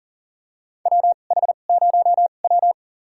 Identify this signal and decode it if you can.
Morse code, heard as a keyed tone, about 40 wpm, spaced as for 29 wpm, 690 Hz. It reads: WH0W
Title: USA Call Signs - 4 Characters - 2x Character Spacing - 40wpm